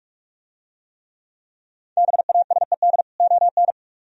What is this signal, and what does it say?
Morse code, heard as a keyed tone, about 44 wpm, 700 Hz. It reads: BASED ON